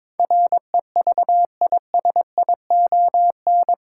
Morse code, heard as a keyed tone, about 22 wpm, 700 Hz. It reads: REVISION
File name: practice / 22wpm / Eight Letter Words - 22wpm